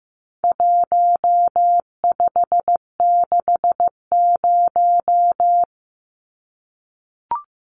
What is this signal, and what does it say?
Morse code, heard as a keyed tone, about 15 wpm, 700 Hz. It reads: 1560
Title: Numbers - Contest Style - 4-Digit 15wpm